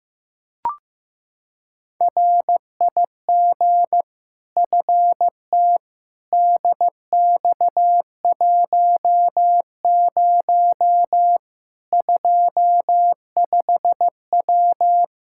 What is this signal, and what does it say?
Morse code, heard as a keyed tone, about 15 wpm, 700 Hz. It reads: RIG FT DX10 25W
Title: K1USN SST QSO Practice 15wpm